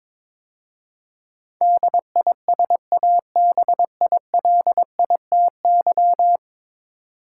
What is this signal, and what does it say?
Morse code, heard as a keyed tone, about 22 wpm, 700 Hz. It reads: DISABILITY
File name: practice / 22wpm / 10 Letter Words - First 3 Letters Given - 22wpm